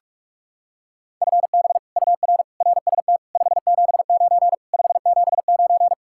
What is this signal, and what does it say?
Morse code, heard as a keyed tone, about 36 wpm, 700 Hz. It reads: FB UR RST 579 579
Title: Speed-Racing Common QSO Phrases - 30wpm